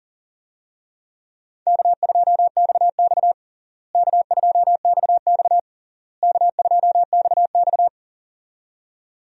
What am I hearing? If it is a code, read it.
Morse code, heard as a keyed tone, about 40 wpm, 700 Hz. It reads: K2XX K2XX K2XX
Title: USA Call Signs - 4 Characters - Repeated 3x - 40wpm